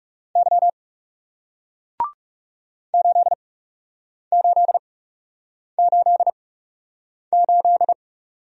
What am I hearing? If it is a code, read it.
Morse code, heard as a keyed tone, about 39 wpm, spaced as for 8 wpm, 700 Hz. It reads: Y 8888